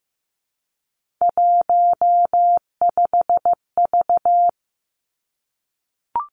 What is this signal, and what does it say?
Morse code, heard as a keyed tone, about 15 wpm, 700 Hz. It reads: 15V
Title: Three Challenging Letter-Number Combinations 15wpm